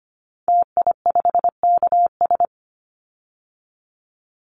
Morse code, heard as a keyed tone, about 25 wpm, 700 Hz. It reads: TI5KS
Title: Call Signs Repeated 3x - 25wpm